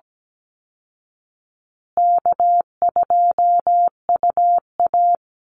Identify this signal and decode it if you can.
Morse code, heard as a keyed tone, about 17 wpm, 700 Hz. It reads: K2UA
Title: Easy Call Signs Repeated 3x - 17wpm